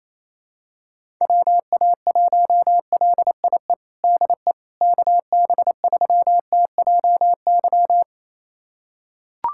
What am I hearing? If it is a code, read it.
Morse code, heard as a keyed tone, about 28 wpm, 700 Hz. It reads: WA1LIE DE KB3TJY
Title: USA Callsign DE Callsign - 6 Characters - 28wpm